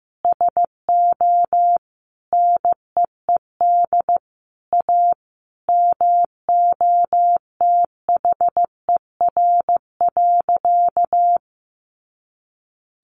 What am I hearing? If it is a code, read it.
Morse code, heard as a keyed tone, about 15 wpm, 700 Hz. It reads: SO NEED A MOTHER.
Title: Sentences from Top 200 Words 15wpm